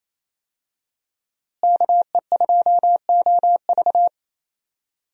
Morse code, heard as a keyed tone, about 28 wpm, 700 Hz. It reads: KE2OV
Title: USA Call Signs - 5 Characters - Repeated 3x - 28wpm